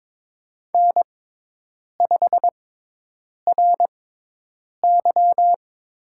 Morse code, heard as a keyed tone, about 23 wpm, spaced as for 8 wpm, 700 Hz. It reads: N5RY
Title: USA Call Signs - 6x Character Spacing - 22wpm